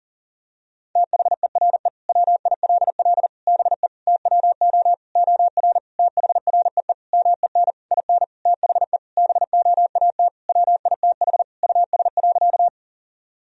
Morse code, heard as a keyed tone, about 40 wpm, 700 Hz. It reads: THERE WILL BE TWO OR THREE MEN IN THE BOAT WITH US.
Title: Sentences from Top 1000 Words - Review Part 3 - 40wpm